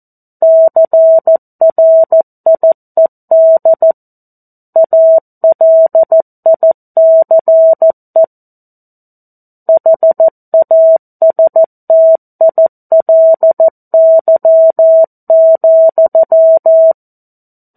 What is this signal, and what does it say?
Morse code, heard as a keyed tone, about 14 wpm, 650 Hz. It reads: CRIED ALICE HASTILY,
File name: 02-Chapter-02 14 WPM